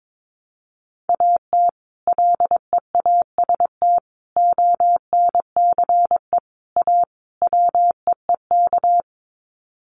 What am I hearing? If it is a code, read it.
Morse code, heard as a keyed tone, about 22 wpm, 700 Hz. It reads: AT LEAST ONCE A WEEK
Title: Sets of 5 Words Encore2 22wpm